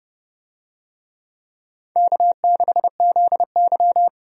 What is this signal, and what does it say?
Morse code, heard as a keyed tone, about 30 wpm, 700 Hz. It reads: K6ZY
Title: USA Call Signs - 4 Characters - 30wpm